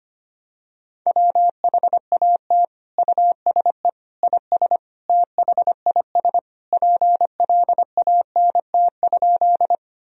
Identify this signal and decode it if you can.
Morse code, heard as a keyed tone, about 25 wpm, 700 Hz. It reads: WHAT USE IS THIS PLANT?
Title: Sentences from Top 1000 Words - Review Part 1 - 25wpm